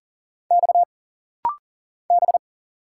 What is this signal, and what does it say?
Morse code, heard as a keyed tone, about 39 wpm, 700 Hz. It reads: X B
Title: Single Letter-Number Alt - Rapid-Fire 40wpm